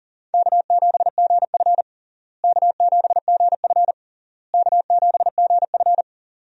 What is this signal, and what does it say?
Morse code, heard as a keyed tone, about 40 wpm, 700 Hz. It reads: K7GF K7GF K7GF
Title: USA Call Signs - 4 Characters - Repeated 3x - 40wpm